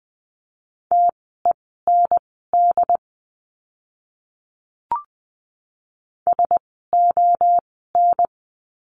Morse code, heard as a keyed tone, about 20 wpm, spaced as for 14 wpm, 700 Hz. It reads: TEND SON
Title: 068 Lesson 12f - Letter d - Words 20wpm